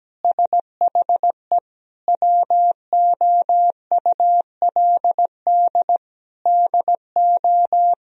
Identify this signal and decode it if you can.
Morse code, heard as a keyed tone, about 17 wpm, 700 Hz. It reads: SHE WOULD DO
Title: Sentences from Top 1000 Words - Review Part 3 - 17wpm